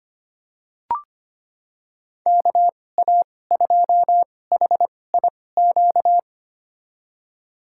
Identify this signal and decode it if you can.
Morse code, heard as a keyed tone, about 25 wpm, spaced as for 18 wpm, 700 Hz. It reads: KA2HIQ
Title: USA Call Signs - 6 Characters - 2x Character Spacing - 25wpm